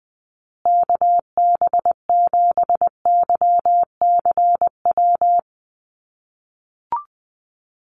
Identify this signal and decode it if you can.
Morse code, heard as a keyed tone, about 20 wpm, 700 Hz. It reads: KB7YCW
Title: USA Call Signs Repeated 2x 20wpm